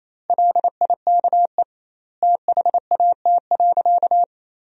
Morse code, heard as a keyed tone, about 28 wpm, 700 Hz. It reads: LIKE THAT.
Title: 2x Word-Spacing - Sentences from Top 500 Words - 28wpm